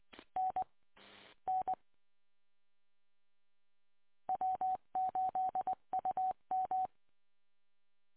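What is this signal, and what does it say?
Morse code, heard as a keyed tone, about 20 wpm, 750 Hz. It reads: N N W8UM
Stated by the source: Recorded from the W8UM audio archive, 145.230.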